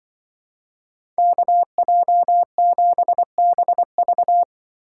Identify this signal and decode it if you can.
Morse code, heard as a keyed tone, about 24 wpm, 700 Hz. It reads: KJ7BV
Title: Speed Racing - USA 2x2 Call Signs 20wpm